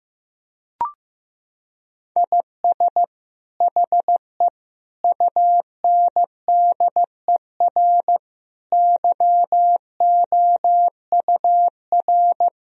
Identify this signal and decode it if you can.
Morse code, heard as a keyed tone, about 15 wpm, 700 Hz. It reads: IS HE UNDER YOUR
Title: Sentences from Top 1000 Words - Review Part 1 - 15wpm